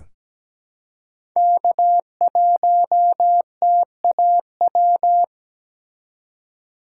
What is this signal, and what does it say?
Morse code, heard as a keyed tone, about 17 wpm, 700 Hz. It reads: K1TAW
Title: USA Call Signs - 5 Characters - 2x Character Spacing - 17wpm